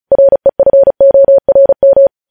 Morse code, heard as a keyed tone, about 36 wpm, 550 Hz. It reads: REFORM
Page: news headlines in morse code at 35 wpm